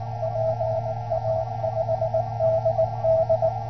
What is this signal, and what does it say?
Morse code, heard as a keyed tone, about 19 wpm, 650 Hz. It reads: PA5DD